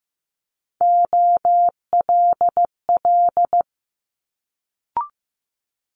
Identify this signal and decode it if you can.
Morse code, heard as a keyed tone, about 15 wpm, 700 Hz. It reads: OLL
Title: Multi-letter Phonemes Quick Fire 15wpm